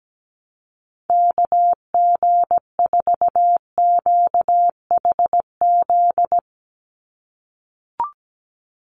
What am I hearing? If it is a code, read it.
Morse code, heard as a keyed tone, about 17 wpm, 700 Hz. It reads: KG4QHZ
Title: USA Call Signs 17wpm